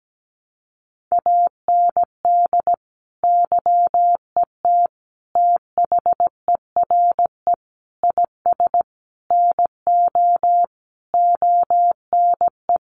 Morse code, heard as a keyed tone, about 17 wpm, 700 Hz. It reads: AND YET THERE IS NO ONE
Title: Sentences from Top 1000 Words 17wpm